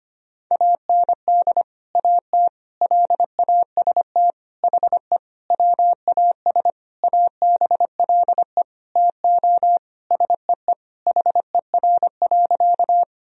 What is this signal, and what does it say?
Morse code, heard as a keyed tone, about 25 wpm, 700 Hz. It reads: AND AT LAST HE WAS ABLE TO SEE HER.
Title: Sentences from Top 1000 Words - Review Part 2 - 25wpm